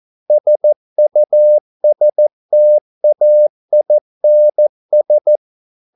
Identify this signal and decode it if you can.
Morse code, heard as a keyed tone, about 14 wpm, 600 Hz. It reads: SUSTAINS